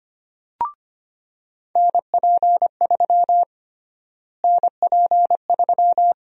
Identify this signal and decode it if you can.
Morse code, heard as a keyed tone, about 25 wpm, 700 Hz. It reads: NP3 NP3
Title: USA Call Sign Prefixes - Repeated 2x - 25wpm